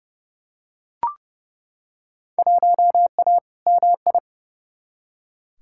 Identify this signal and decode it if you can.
Morse code, heard as a keyed tone, about 30 wpm, 700 Hz. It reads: 1A MI
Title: Field Day Exchanges - 30wpm